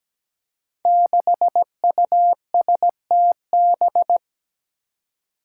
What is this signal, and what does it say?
Morse code, heard as a keyed tone, about 17 wpm, 700 Hz. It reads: 6USTB
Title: Five Challenging Letter-Number Combinations 17wpm